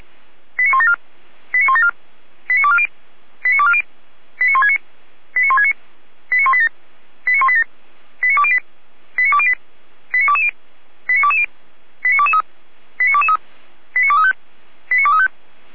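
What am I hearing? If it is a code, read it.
Five-tone selective call (ZVEI), heard as quick runs of five tones, tones of about 80 ms. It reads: 89174 89174 89240 89240 89169 89169 89187 89187 89208 89208 89220 89220 89223 89223 89235 89235
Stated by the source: Five-tone ZVEI call 89168 received on an SDR.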